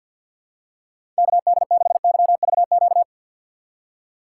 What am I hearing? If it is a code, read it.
Morse code, heard as a keyed tone, about 53 wpm, 700 Hz. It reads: KD6YVQ
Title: USA Call Signs - 6 Characters - 3x Character Spacing - 50wpm